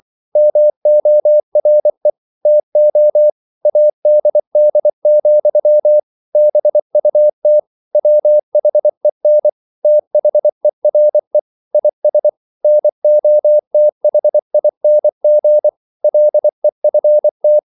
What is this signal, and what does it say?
Morse code, heard as a keyed tone, about 24 wpm, 600 Hz. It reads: MORE TO ADD, BUT WHEN THERE IS NOTHING LEFT
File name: CW / Practice / EnglishQuotations0224